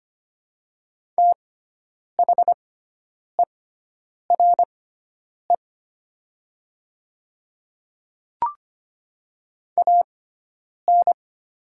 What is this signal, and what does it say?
Morse code, heard as a keyed tone, about 25 wpm, spaced as for 9 wpm, 700 Hz. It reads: THERE AN
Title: 6x Character Spacing - Top 100 Words - 25wpm